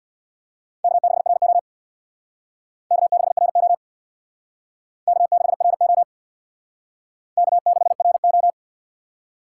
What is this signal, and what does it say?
Morse code, heard as a keyed tone, about 60 wpm, 700 Hz. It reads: K6RO K6RO K6RO K6RO
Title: Speed Racing - USA 1x2 Call Signs 50wpm